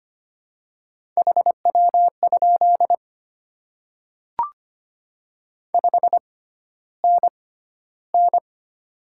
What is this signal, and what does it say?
Morse code, heard as a keyed tone, about 25 wpm, spaced as for 13 wpm, 700 Hz. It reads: HW? 5NN
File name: Top 20 QSO Elements - 6x Character Spacing - 25wpm